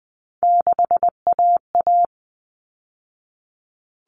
Morse code, heard as a keyed tone, about 20 wpm, 700 Hz. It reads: 6AA
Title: All Cut Numbers - 3-Digit 20wpm